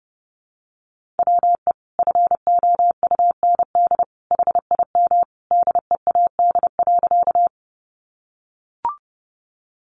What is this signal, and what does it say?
Morse code, heard as a keyed tone, about 30 wpm, 700 Hz. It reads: WE FOUND HIM DEAD.
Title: Sentences from Top 1500 Words 30wpm